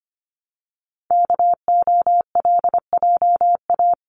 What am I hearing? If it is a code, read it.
Morse code, heard as a keyed tone, about 25 wpm, 700 Hz. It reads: KOLJA
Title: CWT Exchange 25wpm